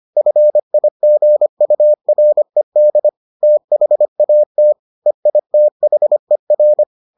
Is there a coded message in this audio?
Yes: Morse code from 0.2 to 6.8 s, heard as a keyed tone, about 25 wpm, 600 Hz. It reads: FIGURED THAT EITHER